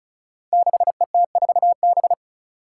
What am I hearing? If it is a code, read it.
Morse code, heard as a keyed tone, about 36 wpm, 700 Hz. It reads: 6ET4B
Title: Five Challenging Letter-Number Combinations 35wpm